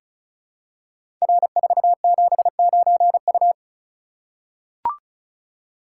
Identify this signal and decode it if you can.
Morse code, heard as a keyed tone, about 36 wpm, 700 Hz. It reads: R479U
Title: Code Groups - 5 Letter:Numbers - 35wpm